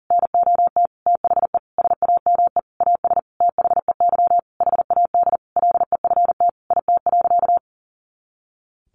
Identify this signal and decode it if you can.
Morse code, heard as a keyed tone, about 40 wpm, 700 Hz. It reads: NOT THE SAME AS THEY HAD LEFT IT.
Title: A Sound Of Thunder 40wpm